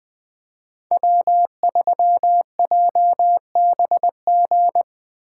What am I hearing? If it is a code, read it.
Morse code, heard as a keyed tone, about 20 wpm, 700 Hz. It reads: W3JBG
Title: Speed Racing - USA 1x3 Call Signs 17wpm